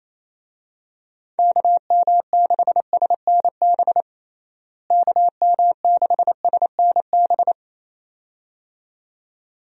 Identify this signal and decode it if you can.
Morse code, heard as a keyed tone, about 28 wpm, 700 Hz. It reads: KM6SNB KM6SNB
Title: USA Call Signs - 6 Characters - Repeated 2x - 28wpm